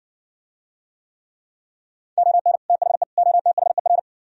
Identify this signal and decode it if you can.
Morse code, heard as a keyed tone, about 60 wpm, 700 Hz. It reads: ON THE OTHER